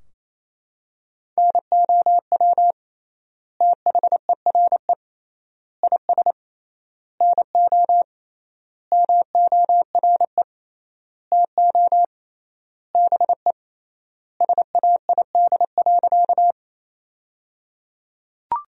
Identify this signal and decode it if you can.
Morse code, heard as a keyed tone, about 27 wpm, 700 Hz. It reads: NOW THERE IS NO MORE TO BE SAID.
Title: 3x Word-Spacing - Sentences from Top 100 Words - 28wpm